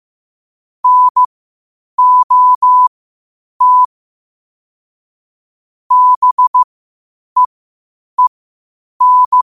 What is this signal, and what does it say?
Morse code, heard as a keyed tone, about 15 wpm, spaced as for 8 wpm, 1000 Hz. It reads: NOT BEEN
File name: qotdfiles / 8wpm